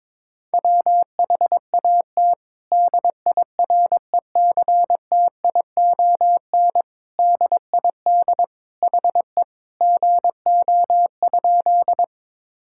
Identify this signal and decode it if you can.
Morse code, heard as a keyed tone, about 22 wpm, 700 Hz. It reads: WHAT DIRECTION DID HE GO?
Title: Sentences from Top 1000 Words - Review Part 5 - 22wpm